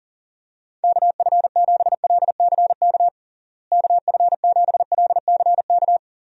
Morse code, heard as a keyed tone, about 40 wpm, 700 Hz. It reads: KF7LCK KF7LCK
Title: USA Call Signs - 6 Characters - Repeated 3x - 40wpm